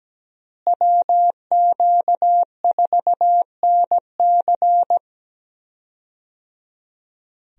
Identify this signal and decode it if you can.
Morse code, heard as a keyed tone, about 17 wpm, 700 Hz. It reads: WQ4NC
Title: USA Call Signs - 5 Characters - Repeated 3x - 17wpm